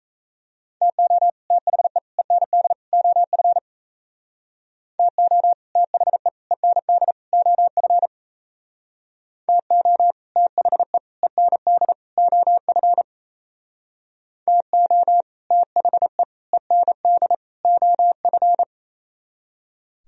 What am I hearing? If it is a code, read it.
Morse code, heard as a keyed tone, about 37 wpm, 700 Hz. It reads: TO THE END OF TO THE END OF TO THE END OF TO THE END OF